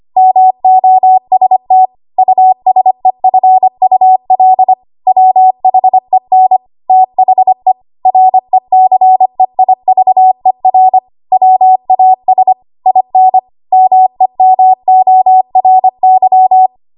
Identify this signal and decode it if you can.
Morse code, heard as a keyed tone, about 25 wpm, 750 Hz. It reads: MOST USEFUL WHEN THE RECEIVER WAS IN MEMORY
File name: Archive / 25 WPM / 25WPM